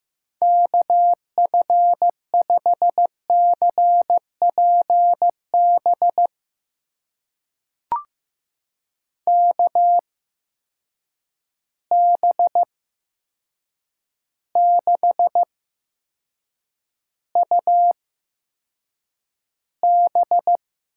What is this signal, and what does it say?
Morse code, heard as a keyed tone, about 15 wpm, spaced as for 7 wpm, 700 Hz. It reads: KF5CPB KB6UB
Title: USA Call Signs - 6 Characters - 8x Character Spacing - 15wpm